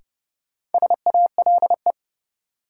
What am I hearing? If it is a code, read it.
Morse code, heard as a keyed tone, about 30 wpm, 700 Hz. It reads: SALE
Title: Sentences from Top 1400 Words - New Word Drill Alt - 30wpm